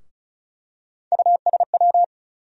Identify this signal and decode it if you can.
Morse code, heard as a keyed tone, about 36 wpm, 700 Hz. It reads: USW
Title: USA Call Sign Postfixes - 3 Letters 35wpm